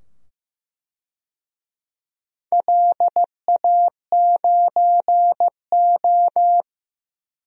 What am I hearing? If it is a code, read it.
Morse code, heard as a keyed tone, about 15 wpm, 700 Hz. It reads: LA9O
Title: DXCC-100 Call Signs - 4 Characters - 15wpm